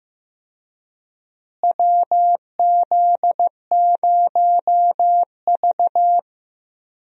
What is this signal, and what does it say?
Morse code, heard as a keyed tone, about 15 wpm, 700 Hz. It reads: WZ0V
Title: USA Call Signs - 4 Characters - Repeated 3x - 15wpm